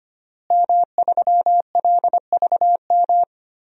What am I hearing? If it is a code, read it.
Morse code, heard as a keyed tone, about 25 wpm, 700 Hz. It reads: M3LVM
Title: DXCC-100 Call Signs - 5 Characters - Repeated 2x - 25wpm